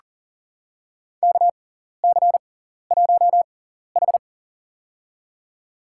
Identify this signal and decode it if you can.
Morse code, heard as a keyed tone, about 40 wpm, spaced as for 14 wpm, 700 Hz. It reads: KC1H
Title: USA Call Signs - 4 Characters - 6x Character Spacing - 40wpm